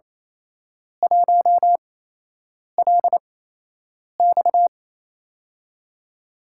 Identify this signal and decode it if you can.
Morse code, heard as a keyed tone, about 28 wpm, spaced as for 8 wpm, 700 Hz. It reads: 1LX
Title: USA Call Signs - 5 Characters - 8x Character Spacing - 28wpm